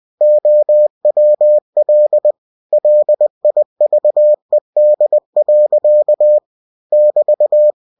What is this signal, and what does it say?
Morse code, heard as a keyed tone, about 20 wpm, 600 Hz. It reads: OWL LIVED. <BT>
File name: cw-books / winnie-the-pooh / winnie-20wpm-page-0028